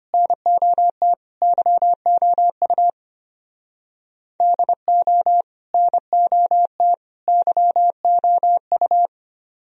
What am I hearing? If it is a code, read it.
Morse code, heard as a keyed tone, about 32 wpm, 700 Hz. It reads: NOT YOU DO NOT YOU